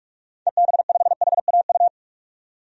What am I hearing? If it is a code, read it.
Morse code, heard as a keyed tone, about 46 wpm, 700 Hz. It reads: EB5HAU
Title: DXCC-100 Call Signs - 6 Characters - 45wpm